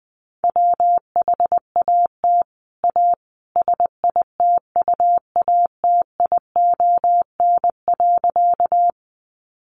Morse code, heard as a keyed tone, about 20 wpm, 700 Hz. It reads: WHAT A SITUATION.
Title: Sentences from Top 1200 Words 20wpm